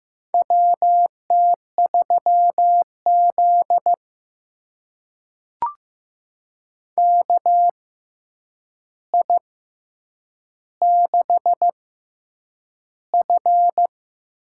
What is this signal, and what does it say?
Morse code, heard as a keyed tone, about 15 wpm, spaced as for 8 wpm, 700 Hz. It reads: WT3Z KI6F
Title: USA Call Signs - 6x Character Spacing - 15wpm